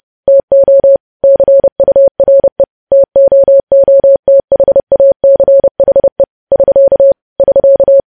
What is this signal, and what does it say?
Morse code, heard as a keyed tone, about 30 wpm, 550 Hz. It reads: TO CURE TOOTHACHE <SK> <SK>
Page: news headlines in morse code at 30 wpm